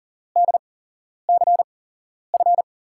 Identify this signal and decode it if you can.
Morse code, heard as a keyed tone, about 40 wpm, spaced as for 11 wpm, 700 Hz. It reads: DCF
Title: USA Call Signs - 8x Character Spacing - 40wpm